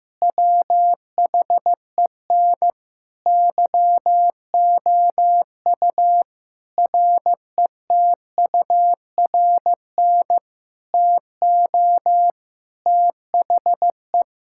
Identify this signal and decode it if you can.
Morse code, heard as a keyed tone, about 15 wpm, 700 Hz. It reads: WHEN YOU RETURN TO THE
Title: Sentences from Top 1600 Words 15wpm